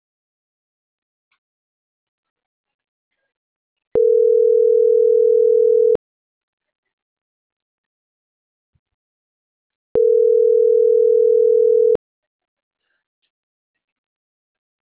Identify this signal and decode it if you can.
Ringback tone (North America), 440 + 480 Hz.